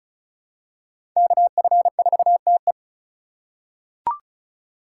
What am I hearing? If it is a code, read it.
Morse code, heard as a keyed tone, about 34 wpm, 700 Hz. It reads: KF4TE
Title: USA Call Signs - 5 Characters - 2x Character Spacing - 35wpm